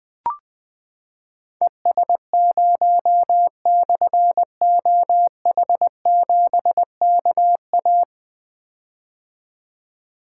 Morse code, heard as a keyed tone, about 20 wpm, 700 Hz. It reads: ES0/OH7KA
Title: Difficult Call Signs 20wpm